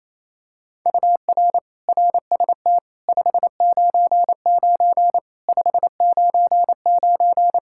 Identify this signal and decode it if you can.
Morse code, heard as a keyed tone, about 28 wpm, 700 Hz. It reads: UR RST 599 599
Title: Practice QSO - Exchange 1 - Beginning - 28wpm